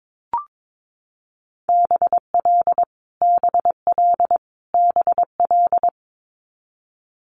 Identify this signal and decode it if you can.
Morse code, heard as a keyed tone, about 22 wpm, 700 Hz. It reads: BL BL BL
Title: Multi-letter Phonemes Repeated 3x 22wpm